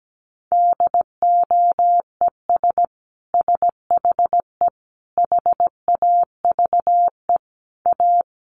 Morse code, heard as a keyed tone, about 17 wpm, 700 Hz. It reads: DOES SHE HAVE A
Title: Sentences from Top 1400 Words 17wpm